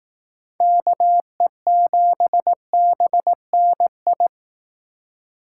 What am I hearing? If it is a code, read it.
Morse code, heard as a keyed tone, about 18 wpm, 700 Hz. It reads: KE7BNI
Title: Speed Racing - USA 2x3 Call Signs 15wpm